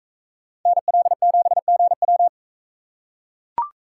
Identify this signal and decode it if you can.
Morse code, heard as a keyed tone, about 42 wpm, 700 Hz. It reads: NL7GW